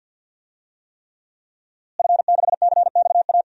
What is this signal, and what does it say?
Morse code, heard as a keyed tone, about 52 wpm, 700 Hz. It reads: F6CXA
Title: DXCC-100 Call Signs - 5 Characters - Repeated 3x - 50wpm